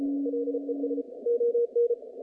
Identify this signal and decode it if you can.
Morse code, heard as a keyed tone, about 32 wpm, 470 Hz. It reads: LES ON
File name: contests / naqp-cw-k5tr-op-k5pi- -0000